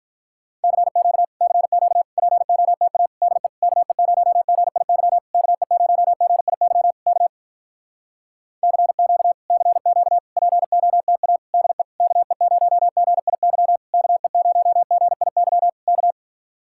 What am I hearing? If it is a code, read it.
Morse code, heard as a keyed tone, about 48 wpm, 700 Hz. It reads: CQ CQ POTA DE KE0GIY KE0GIY K CQ CQ POTA DE KE0GIY KE0GIY K